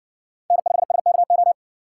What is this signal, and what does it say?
Morse code, heard as a keyed tone, about 60 wpm, 700 Hz. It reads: N5SKO